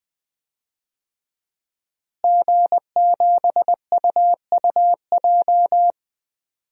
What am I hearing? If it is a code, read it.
Morse code, heard as a keyed tone, about 20 wpm, 700 Hz. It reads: G7UUJ
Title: DXCC-100 Call Signs - 5 Characters - Repeated 2x - 20wpm